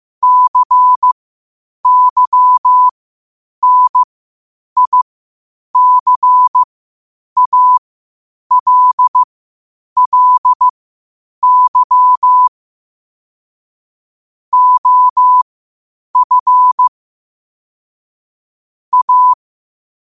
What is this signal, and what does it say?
Morse code, heard as a keyed tone, about 15 wpm, spaced as for 8 wpm, 1000 Hz. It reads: CYNICALLY OF A
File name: qotdfiles / 8wpm